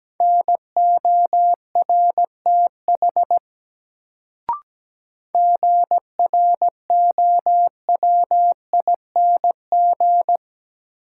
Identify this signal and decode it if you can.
Morse code, heard as a keyed tone, about 17 wpm, 700 Hz. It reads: NORTH GROWING
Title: Sets of 5 Words Encore - New Word Drill - 17wpm